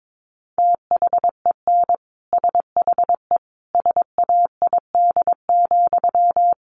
Morse code, heard as a keyed tone, about 22 wpm, 700 Hz. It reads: THEN SHE SAID,
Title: Sentences from Top 800 Words 22wpm